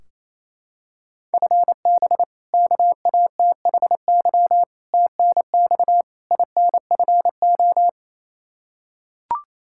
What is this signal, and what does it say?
Morse code, heard as a keyed tone, about 28 wpm, 700 Hz. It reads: FB KATHY TNX INFO
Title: Practice QSO - Exchange 3 - More Info - 28wpm